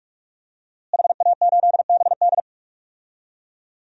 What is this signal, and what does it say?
Morse code, heard as a keyed tone, about 42 wpm, 700 Hz. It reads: HA8BD
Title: DXCC-100 Call Signs - 5 Characters - Repeated 2x - 45wpm